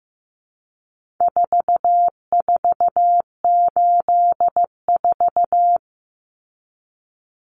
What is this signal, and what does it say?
Morse code, heard as a keyed tone, about 15 wpm, 700 Hz. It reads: 4484
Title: Common Cut Numbers - 4-Digit 15wpm